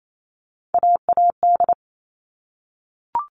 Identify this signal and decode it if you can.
Morse code, heard as a keyed tone, about 28 wpm, 700 Hz. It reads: AAD
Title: All Cut Numbers - 3-Digit 28wpm